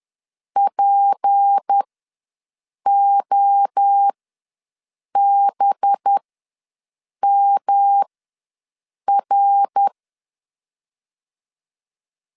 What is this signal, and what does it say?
Morse code, heard as a keyed tone, about 10 wpm, spaced as for 6 wpm, 800 Hz. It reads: POBMR
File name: LEZIONI / 6 PKX0 / 6 P ws 10wpm 3751